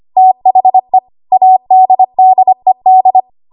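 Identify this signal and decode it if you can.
Morse code, heard as a keyed tone, about 25 wpm, 750 Hz. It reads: THE ADDED
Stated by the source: ARRL 25 WPM code practice archive.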